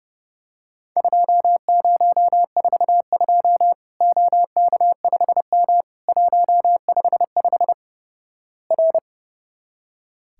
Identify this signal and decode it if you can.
Morse code, heard as a keyed tone, about 32 wpm, 700 Hz. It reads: 2042 OK5M 155 R
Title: Worked All Europe DX Contest - QTC Exchange - 30wpm